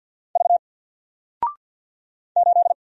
Morse code, heard as a keyed tone, about 50 wpm, 700 Hz. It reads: V 8
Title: Single Character (Letter:Number:Punctuation:Prosign) 50wpm